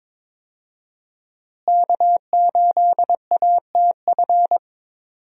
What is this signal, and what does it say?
Morse code, heard as a keyed tone, about 22 wpm, 700 Hz. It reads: K8ATF
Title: USA Call Signs - 5 Characters - Repeated 2x - 22wpm